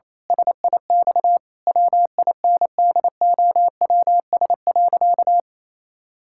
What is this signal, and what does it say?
Morse code, heard as a keyed tone, about 28 wpm, 700 Hz. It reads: SIX WINDOWS.
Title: Pangram Challenge v2 - 28wpm